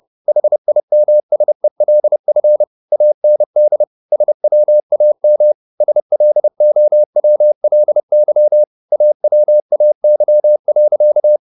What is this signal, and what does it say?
Morse code, heard as a keyed tone, about 30 wpm, 600 Hz. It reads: HIMSELF AND SWAM SLOWLY AWAY.